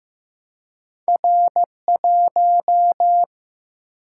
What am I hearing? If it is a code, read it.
Morse code, heard as a keyed tone, about 15 wpm, 700 Hz. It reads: R1
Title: Single-Letter Single-Number 15wpm